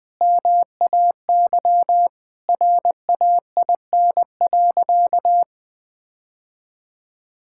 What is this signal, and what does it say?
Morse code, heard as a keyed tone, about 20 wpm, 700 Hz. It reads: MAY RAIN.
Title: Sentences with 8-Word Phrases - 20wpm